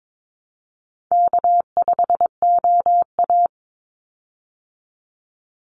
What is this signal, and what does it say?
Morse code, heard as a keyed tone, about 22 wpm, 700 Hz. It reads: K5OA
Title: Easy Call Signs Repeated 3x - 22wpm